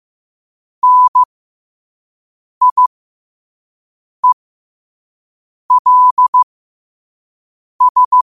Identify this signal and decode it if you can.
Morse code, heard as a keyed tone, about 15 wpm, spaced as for 5 wpm, 1000 Hz. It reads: NIELS
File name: qotdfiles / 5wpm